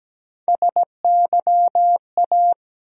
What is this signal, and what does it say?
Morse code, heard as a keyed tone, about 17 wpm, 700 Hz. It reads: SYA
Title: USA Call Sign Postfixes - 3 Letters 17wpm